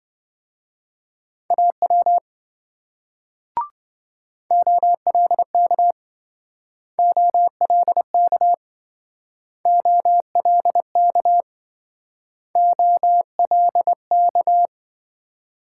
Morse code, heard as a keyed tone, about 27 wpm, 700 Hz. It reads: AW OLK OLK OLK OLK